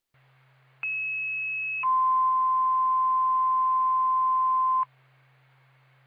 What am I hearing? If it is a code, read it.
Two-tone page, heard as a one-second tone, then a three-second tone, about 2600 then 1040 Hz. It is heard from 0.8 to 4.8 s.